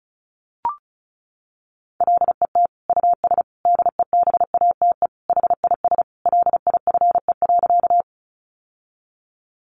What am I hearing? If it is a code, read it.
Morse code, heard as a keyed tone, about 34 wpm, 700 Hz. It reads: LET US DEBATE HIS LIFE.
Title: Sentences from Top 1100 Words 35wpm